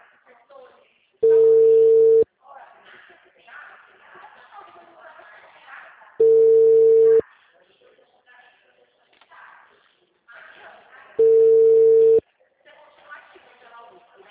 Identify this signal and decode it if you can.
Ringback tone (China, 450 Hz), with speech.
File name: out- -2029- - - .32616